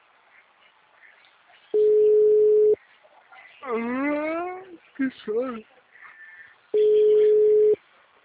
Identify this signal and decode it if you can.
Ringback tone (Europe, 425 Hz), with speech.